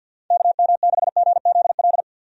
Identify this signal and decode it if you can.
Morse code, heard as a keyed tone, about 46 wpm, 700 Hz. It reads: KM6GZL
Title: USA Call Signs - 6 Characters - 50wpm